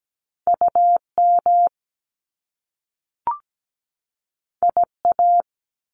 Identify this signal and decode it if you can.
Morse code, heard as a keyed tone, about 17 wpm, 700 Hz. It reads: UM IA
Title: Top Two Letter Combinations in Words 17wpm